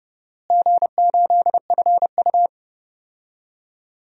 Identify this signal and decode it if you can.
Morse code, heard as a keyed tone, about 30 wpm, 700 Hz. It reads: G8FU
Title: DXCC-100 Call Signs - 4 Characters - Repeated 3x - 30wpm